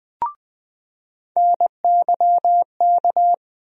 Morse code, heard as a keyed tone, about 20 wpm, 700 Hz. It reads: NYK
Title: USA Call Sign Postfixes - 3 Letters 20wpm